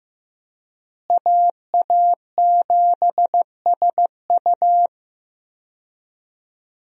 Morse code, heard as a keyed tone, about 15 wpm, 700 Hz. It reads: AA7SU
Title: USA Call Signs - 5 Characters - Repeated 3x - 15wpm